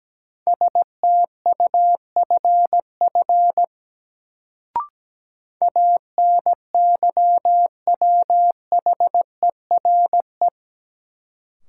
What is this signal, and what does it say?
Morse code, heard as a keyed tone, about 17 wpm, 700 Hz. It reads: STUFF ANYWHERE
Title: Sentences from Top 1400 Words - New Word Drill - 17wpm